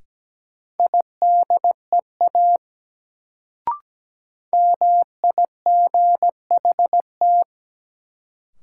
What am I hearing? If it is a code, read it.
Morse code, heard as a keyed tone, about 17 wpm, 700 Hz. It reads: IDEA MIGHT
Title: Sentences from Top 300 Words - New Word Drill - 17wpm